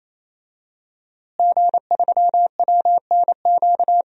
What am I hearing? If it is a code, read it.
Morse code, heard as a keyed tone, about 28 wpm, 700 Hz. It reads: G3WNQ
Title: DXCC-100 Call Signs - 5 Characters - Repeated 3x - 28wpm